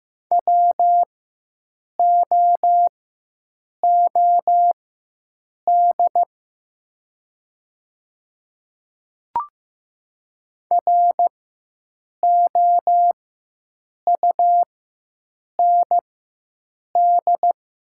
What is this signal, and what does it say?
Morse code, heard as a keyed tone, about 15 wpm, spaced as for 7 wpm, 700 Hz. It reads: WOOD ROUND
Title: 4x Character Spacing - Top 300 Words - 15wpm